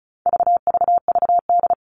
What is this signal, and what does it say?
Morse code, heard as a keyed tone, about 34 wpm, 700 Hz. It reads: VVVD
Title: All Cut Numbers - 4-Digit 35wpm